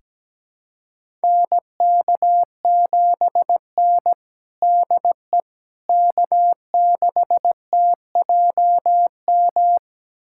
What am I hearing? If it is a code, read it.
Morse code, heard as a keyed tone, about 17 wpm, 700 Hz. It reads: NK7N DE K6TJM
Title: USA Callsign DE Callsign 17wpm